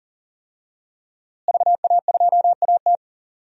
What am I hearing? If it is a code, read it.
Morse code, heard as a keyed tone, about 40 wpm, 700 Hz. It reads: VA2AT
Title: DXCC-100 Call Signs - 5 Characters - Repeated 2x - 40wpm